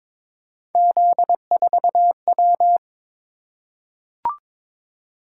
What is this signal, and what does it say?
Morse code, heard as a keyed tone, about 22 wpm, 700 Hz. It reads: Z4W
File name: practice / 22wpm / Three Letter-Number Combinations - 22wpm